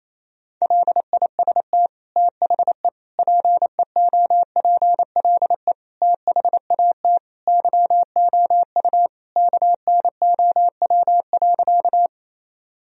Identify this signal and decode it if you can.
Morse code, heard as a keyed tone, about 28 wpm, 700 Hz. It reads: LIST THE PEOPLE THAT YOU KNOW.
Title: Sentences from Top 1000 Words - Review Part 2 - 28wpm